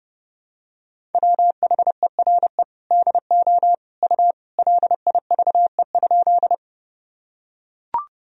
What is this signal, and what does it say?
Morse code, heard as a keyed tone, about 30 wpm, 700 Hz. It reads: WHERE DO U LIVE?
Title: Sentences from Top 200 Words and Abbreviations 30wpm